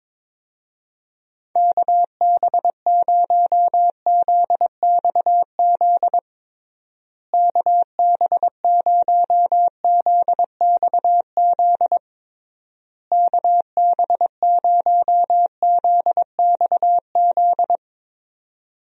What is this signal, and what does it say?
Morse code, heard as a keyed tone, about 22 wpm, 700 Hz. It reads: KB0ZXZ KB0ZXZ KB0ZXZ
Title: USA Call Signs - 6 Characters - Repeated 3x - 22wpm